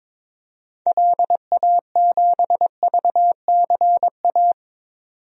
Morse code, heard as a keyed tone, about 22 wpm, 700 Hz. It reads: LA7VCA
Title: DXCC-100 Call Signs - 6 Characters - 22wpm